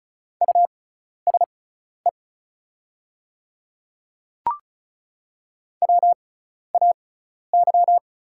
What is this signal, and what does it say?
Morse code, heard as a keyed tone, about 36 wpm, spaced as for 12 wpm, 690 Hz. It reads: USE WAY
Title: 6x Character Spacing - Top 100 Words - 35wpm